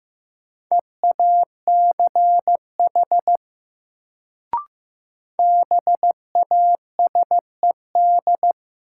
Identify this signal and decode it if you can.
Morse code, heard as a keyed tone, about 15 wpm, 700 Hz. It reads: EACH BASED
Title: Sets of 2 Words - New Word Drill - 15wpm